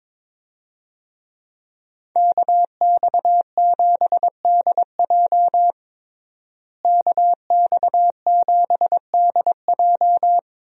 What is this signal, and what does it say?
Morse code, heard as a keyed tone, about 21 wpm, 700 Hz. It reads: KX7DJ KX7DJ
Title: USA Call Signs - 5 Characters - Repeated 3x - 22wpm